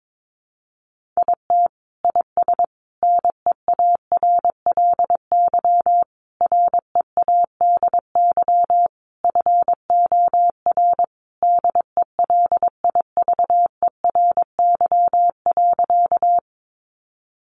Morse code, heard as a keyed tone, about 22 wpm, 700 Hz. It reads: IT IS NEARLY READY FOR DELIVERY.
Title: Sentences from Top 1500 Words 22wpm